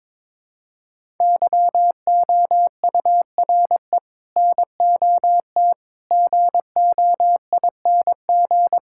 Morse code, heard as a keyed tone, about 21 wpm, 700 Hz. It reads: YOURE NOT GOING
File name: practice / 22wpm / Sets of 7 Words - 22wpm